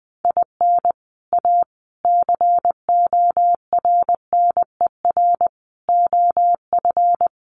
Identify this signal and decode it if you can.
Morse code, heard as a keyed tone, about 20 wpm, 700 Hz. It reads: IN A CORNER OF
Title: Sets of 5 Words Encore2 20wpm